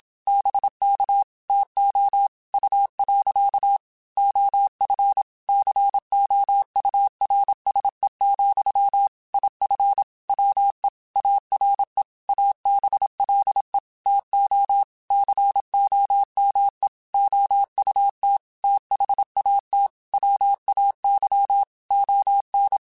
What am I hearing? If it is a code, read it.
Morse code, heard as a keyed tone, about 27 wpm, 800 Hz. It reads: BK TO U. OF COURSE, IF WE ARE ABLE TO COME OUT THAT WAY ON